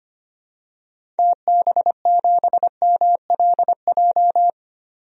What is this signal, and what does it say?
Morse code, heard as a keyed tone, about 25 wpm, 690 Hz. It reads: TB7MLJ
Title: DXCC-100 Call Signs - 6 Characters - 25wpm